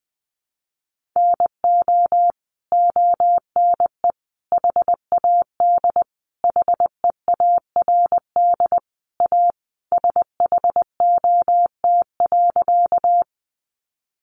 Morse code, heard as a keyed tone, about 20 wpm, 700 Hz. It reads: NO ONE HAD HEARD A SHOT.
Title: Sentences from Top 1100 Words 20wpm